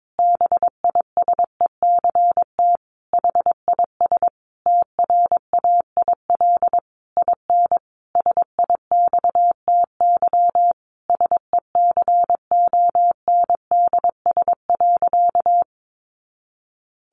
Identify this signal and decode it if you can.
Morse code, heard as a keyed tone, about 22 wpm, 700 Hz. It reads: BISECT HIS TRAIL IN SIXTY SECONDS.
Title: A Sound Of Thunder 22wpm